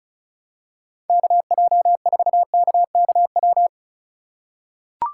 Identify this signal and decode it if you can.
Morse code, heard as a keyed tone, about 36 wpm, 700 Hz. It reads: KJ4KKW
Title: USA Call Signs - 6 Characters - 4x Character Spacing - 35wpm